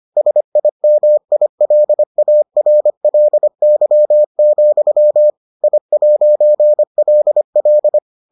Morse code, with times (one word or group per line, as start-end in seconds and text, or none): 0.2-5.3 SIMILARLY,
5.6-8.0 I'LL